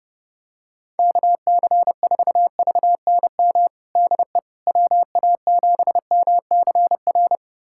Morse code, heard as a keyed tone, about 32 wpm, 700 Hz. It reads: KC4VNM DE WA7MCR
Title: USA Callsign DE Callsign - 6 Characters - 30wpm